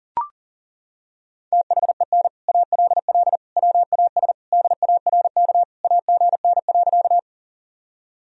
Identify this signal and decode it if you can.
Morse code, heard as a keyed tone, about 37 wpm, 700 Hz. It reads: THEN ALL WAS DARK AGN.
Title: Sentences from Top 500 Words and Abbreviations 40wpm